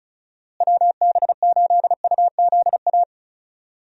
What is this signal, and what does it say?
Morse code, heard as a keyed tone, about 34 wpm, 700 Hz. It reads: WB8UZA
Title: USA Call Signs - 6 Characters - Repeated 2x - 35wpm